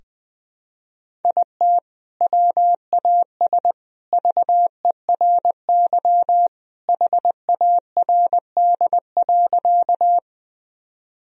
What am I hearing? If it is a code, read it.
Morse code, heard as a keyed tone, about 20 wpm, 700 Hz. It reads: IT WAS VERY HARD.
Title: Sentences from Top 1000 Words - Review Part 1 - 20wpm